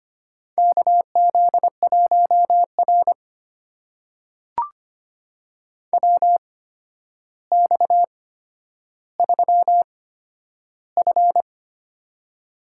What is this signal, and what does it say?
Morse code, heard as a keyed tone, about 25 wpm, spaced as for 11 wpm, 700 Hz. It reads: KZ1R WX3F
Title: USA Call Signs - 4 Characters - 8x Character Spacing - 25wpm